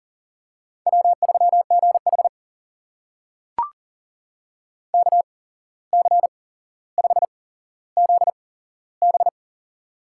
Morse code, heard as a keyed tone, about 40 wpm, spaced as for 16 wpm, 700 Hz. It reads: W3GH KC5ZB
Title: USA Call Signs - 8x Character Spacing - 40wpm